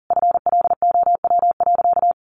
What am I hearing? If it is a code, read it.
Morse code, heard as a keyed tone, about 40 wpm, 700 Hz. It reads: FLOW.
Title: Sentences from Top 1200 Words 40wpm